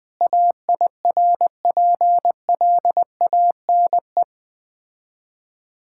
Morse code, heard as a keyed tone, about 20 wpm, 690 Hz. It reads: AIRPLANE
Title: Difficult Words with Hints - 20wpm